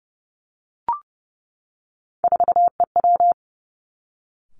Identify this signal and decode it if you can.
Morse code, heard as a keyed tone, about 30 wpm, 700 Hz. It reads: 4EW
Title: Three Challenging Letter-Number Combinations 30wpm